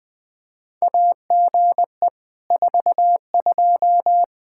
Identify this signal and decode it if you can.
Morse code, heard as a keyed tone, about 20 wpm, 700 Hz. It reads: AGE 42
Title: Practice QSO - Exchange 3 - More Info - 20wpm